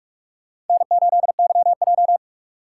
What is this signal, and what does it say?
Morse code, heard as a keyed tone, about 45 wpm, 700 Hz. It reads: N8YJ
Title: USA Call Signs - 3x Character Spacing - 45wpm